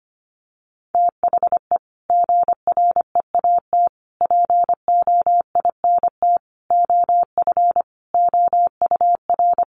Morse code, heard as a keyed tone, about 25 wpm, 700 Hz. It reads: THE GREAT POINT OF OUR
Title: Sentences from Top 1500 Words 25wpm